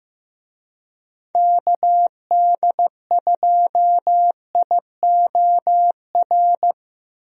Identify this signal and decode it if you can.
Morse code, heard as a keyed tone, about 15 wpm, 700 Hz. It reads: KD2IOR
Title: USA Call Signs - 6 Characters - Repeated 2x - 15wpm